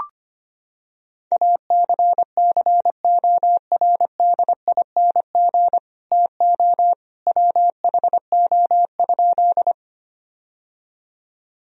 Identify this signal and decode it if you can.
Morse code, heard as a keyed tone, about 25 wpm, 700 Hz. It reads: ACCORDING TO WHO?
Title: Sentences from Top 1000 Words - Review Part 4 - 25wpm